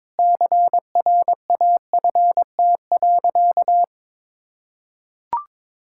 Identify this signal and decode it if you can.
Morse code, heard as a keyed tone, about 22 wpm, 700 Hz. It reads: CRAFT.
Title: Sentences from Top 1000 Words - Review Part 6 - 22wpm